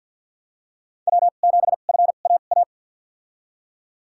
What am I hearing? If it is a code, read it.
Morse code, heard as a keyed tone, about 51 wpm, spaced as for 36 wpm, 700 Hz. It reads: W7FAA
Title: USA Call Signs - 5 Characters - 2x Character Spacing - 50wpm